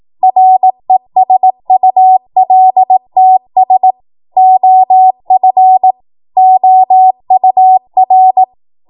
Morse code, heard as a keyed tone, about 18 wpm, 750 Hz. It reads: RESULTS OF OUR
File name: Morse / 18WPM